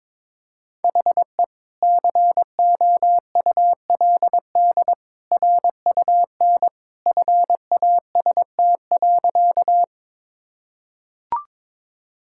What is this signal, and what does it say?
Morse code, heard as a keyed tone, about 22 wpm, 700 Hz. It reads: HE COULD RUN FAST.
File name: practice / 22wpm / Sentences with 2-Word Phrases - 22wpm